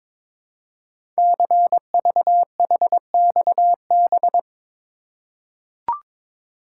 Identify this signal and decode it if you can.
Morse code, heard as a keyed tone, about 22 wpm, 700 Hz. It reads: CVHXB
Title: Code Groups - 5 Letters - 22wpm